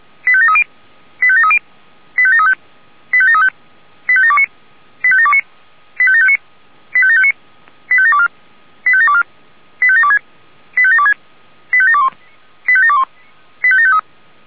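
Five-tone selective call (ZVEI), heard as quick runs of five tones, tones of about 70 ms. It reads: 85730 85730 85735 85735 85729 85729 85759 85759 85724 85724 85726 85726 85721 85721 85753
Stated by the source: ZVEI five-tone call 85722 received by SDR.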